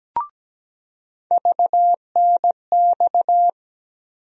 Morse code, heard as a keyed tone, about 17 wpm, 700 Hz. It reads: VNX
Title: USA Call Sign Postfixes - 3 Letters 17wpm